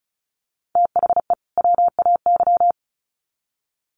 Morse code, heard as a keyed tone, about 34 wpm, 700 Hz. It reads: THE WAY
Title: Sets of 2 Words 35wpm